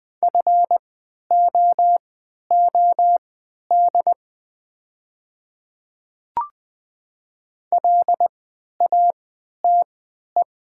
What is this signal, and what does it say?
Morse code, heard as a keyed tone, about 20 wpm, spaced as for 11 wpm, 700 Hz. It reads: FOOD LATE
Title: 3x Character Spacing - Top 500 Words - 20wpm